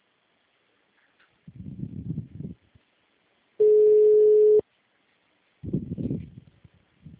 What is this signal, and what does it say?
Ringback tone (Europe, 425 Hz).